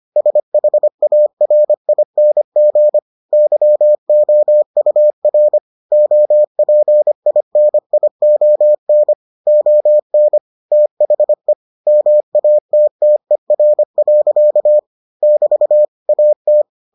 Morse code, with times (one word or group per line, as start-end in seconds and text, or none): 0.2-3.0 SHARING
3.3-5.6 YOUR
5.9-9.1 OPINION
9.5-10.4 ON
10.7-11.5 THE
11.9-14.8 MATTER.
15.2-15.8 <BT>
16.1-16.6 AT